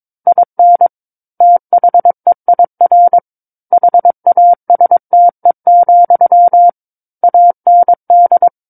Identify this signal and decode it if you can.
Morse code, heard as a keyed tone, about 22 wpm, 700 Hz. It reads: IN THEIR HASTE, AND
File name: Book One / 13-Chapter 13 - 22 WPM